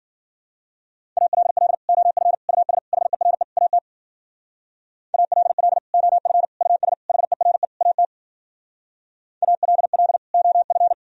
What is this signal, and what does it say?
Morse code, heard as a keyed tone, about 53 wpm, 700 Hz. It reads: ALL OF US HERE AT ALL OF US HERE AT ALL OF